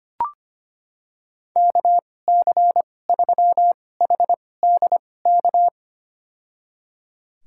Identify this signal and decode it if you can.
Morse code, heard as a keyed tone, about 25 wpm, spaced as for 18 wpm, 700 Hz. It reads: KC3HDK
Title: USA Call Signs - 6 Characters - 2x Character Spacing - 25wpm